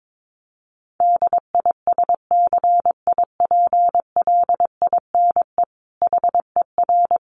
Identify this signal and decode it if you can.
Morse code, heard as a keyed tone, about 22 wpm, 700 Hz. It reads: DISCIPLINE HER
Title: Sentences from Top 900 Words 22wpm